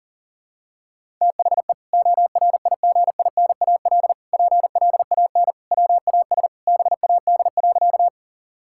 Morse code, heard as a keyed tone, about 40 wpm, 700 Hz. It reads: THE ORIGINAL PLAN WAS BAD.
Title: Sentences from Top 1000 Words - Review Part 6 - 40wpm